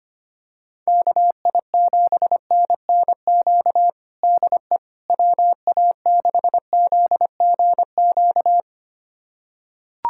Morse code, heard as a keyed tone, about 25 wpm, 700 Hz. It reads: KI7NNQ DE WA6ZGQ
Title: USA Callsign DE Callsign - 6 Characters - 25wpm